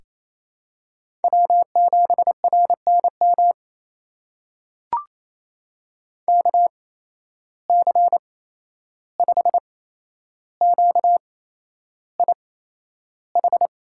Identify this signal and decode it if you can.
Morse code, heard as a keyed tone, about 28 wpm, spaced as for 11 wpm, 700 Hz. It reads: W7RNM KC5QIH
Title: USA Call Signs - 8x Character Spacing - 28wpm